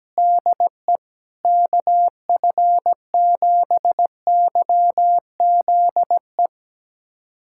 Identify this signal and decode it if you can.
Morse code, heard as a keyed tone, about 17 wpm, 700 Hz. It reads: DE KF7YZE
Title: USA Callsign DE Callsign 17wpm